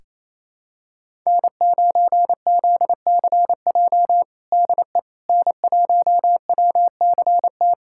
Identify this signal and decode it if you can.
Morse code, heard as a keyed tone, about 27 wpm, 700 Hz. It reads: N9ZCJ DE N1WCT
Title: USA Callsign DE Callsign - 5 Characters - 28wpm